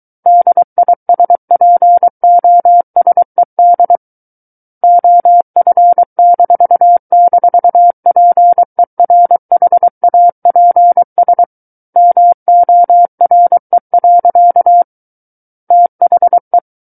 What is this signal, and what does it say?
Morse code, heard as a keyed tone, about 23 wpm, 700 Hz. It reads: DISPOSED OF--PERHAPS MORE. THE
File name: Book One / 14-Chapter 14 - 23 WPM